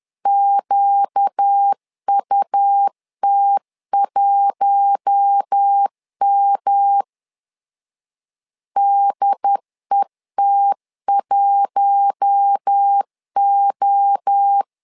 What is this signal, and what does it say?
Morse code, heard as a keyed tone, about 10 wpm, 800 Hz. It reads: QUT1M DET1O
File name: LEZIONI / 4 Q1S 10wpm 2133